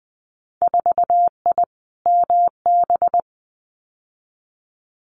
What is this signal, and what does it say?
Morse code, heard as a keyed tone, about 20 wpm, 700 Hz. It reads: 4I MB
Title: Winter Field Day Exchanges Encore 20wpm